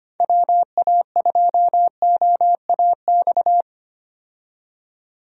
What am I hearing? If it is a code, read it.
Morse code, heard as a keyed tone, about 25 wpm, 700 Hz. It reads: WA2OAX
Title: USA Call Signs - 6 Characters - 25wpm